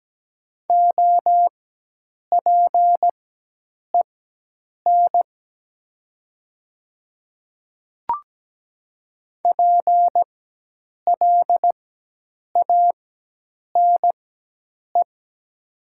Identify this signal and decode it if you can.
Morse code, heard as a keyed tone, about 17 wpm, spaced as for 8 wpm, 700 Hz. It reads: OPEN PLANE
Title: 4x Character Spacing - Top 500 Words - 17wpm